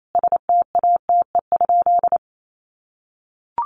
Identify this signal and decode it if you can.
Morse code, heard as a keyed tone, about 28 wpm, 700 Hz. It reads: STATE?
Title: Sentences from Top 1200 Words 28wpm